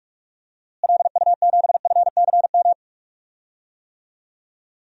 Morse code, heard as a keyed tone, about 43 wpm, 700 Hz. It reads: LU7FCM
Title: DXCC-100 Call Signs - 6 Characters - Repeated 2x - 45wpm